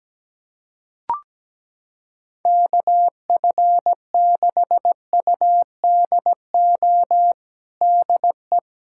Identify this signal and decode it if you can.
Morse code, heard as a keyed tone, about 17 wpm, 700 Hz. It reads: KF6UDO DE
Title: USA Callsign DE Callsign 17wpm